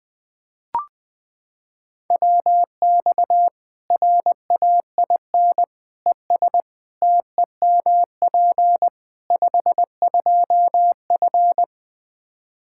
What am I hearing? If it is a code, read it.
Morse code, heard as a keyed tone, about 20 wpm, 700 Hz. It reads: WX RAIN ES TEMP 52F
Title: Practice QSO - Exchange 2 - Info - 20wpm